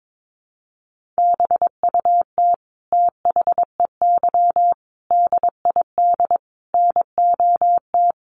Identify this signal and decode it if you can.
Morse code, heard as a keyed tone, about 22 wpm, 700 Hz. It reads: BUT THEY DID NOT
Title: Sentences from Top 1400 Words 22wpm